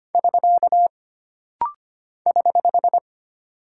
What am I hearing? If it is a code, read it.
Morse code, heard as a keyed tone, about 25 wpm, 700 Hz. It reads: <SK> <HH>
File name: Single Punctuation-Prosign - Rapid-Fire - 25wpm